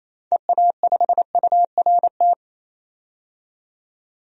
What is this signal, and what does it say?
Morse code, heard as a keyed tone, about 27 wpm, 700 Hz. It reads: EA5URT
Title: DXCC-100 Call Signs - 6 Characters - 28wpm